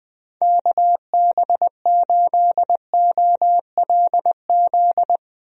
Morse code, heard as a keyed tone, about 20 wpm, 700 Hz. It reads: KB8OLZ
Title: USA Call Signs - 6 Characters - Repeated 3x - 20wpm